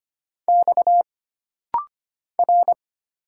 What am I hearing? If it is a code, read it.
Morse code, heard as a keyed tone, about 25 wpm, 700 Hz. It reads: X R
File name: Single Character (Letter:Number:Punctuation:Prosign) - Rapid-Fire - 25wpm